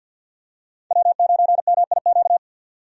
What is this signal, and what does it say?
Morse code, heard as a keyed tone, about 50 wpm, 700 Hz. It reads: W9MIQ
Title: USA Call Signs - 5 Characters - 50wpm